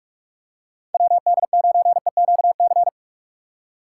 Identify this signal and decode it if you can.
Morse code, heard as a keyed tone, about 45 wpm, 700 Hz. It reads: WD9EQC
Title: USA Call Signs - 6 Characters - 3x Character Spacing - 45wpm